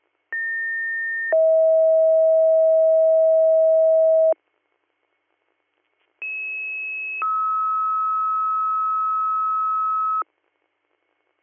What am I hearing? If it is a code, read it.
Two-tone page, heard as a one-second tone, then a three-second tone, about 1800 then 640 Hz, about 2600 then 1290 Hz. It is heard from 0.3 to 10.2 s.